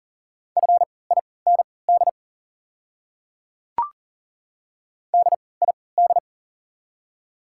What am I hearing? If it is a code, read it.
Morse code, heard as a keyed tone, about 40 wpm, spaced as for 23 wpm, 690 Hz. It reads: FIND DID
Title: 3x Character Spacing - Top 100 Words - 40wpm